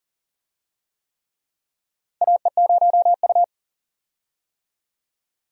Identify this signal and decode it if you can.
Morse code, heard as a keyed tone, about 43 wpm, 690 Hz. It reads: AE0U
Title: USA Call Signs - 4 Characters - 40wpm